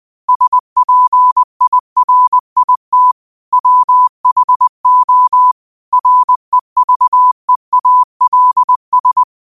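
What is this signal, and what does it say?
Morse code, heard as a keyed tone, about 20 wpm, 1000 Hz. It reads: SPIRIT WHO REVEALS
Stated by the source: Quote of the day sent in Morse at 20 words per minute.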